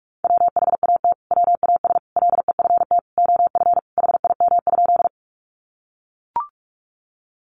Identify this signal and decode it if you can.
Morse code, heard as a keyed tone, about 44 wpm, 700 Hz. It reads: WHAT WAS LEFT OF HIM?
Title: Sentences from Top 300 Words 45wpm